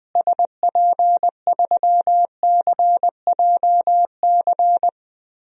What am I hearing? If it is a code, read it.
Morse code, heard as a keyed tone, about 20 wpm, 700 Hz. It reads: SP3CJC
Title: DXCC-100 Call Signs - 6 Characters - Repeated 3x - 20wpm